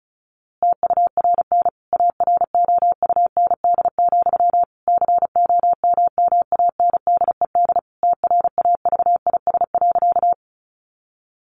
Morse code, heard as a keyed tone, about 34 wpm, 700 Hz. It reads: TURN AROUND, COMMANDED TRAVIS.
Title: A Sound Of Thunder 35wpm